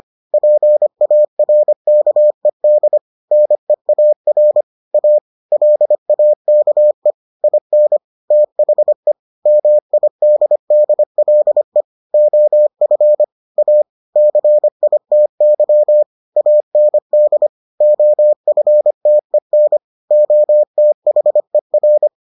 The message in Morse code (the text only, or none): PARKED NEAR A LAKE IN THE MIDDLE OF A CITY AND OFTEN OTHER